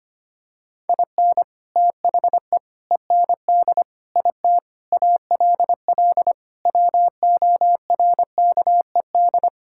Morse code, heard as a keyed tone, about 25 wpm, 700 Hz. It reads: IN THE END IT ALL WORKED
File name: practice / 25wpm / Sentences with 3-Word Phrases - 25wpm